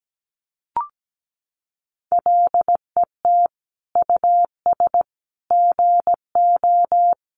Morse code, heard as a keyed tone, about 17 wpm, 700 Hz. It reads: LET US GO
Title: Sentences from Top 1100 Words 17wpm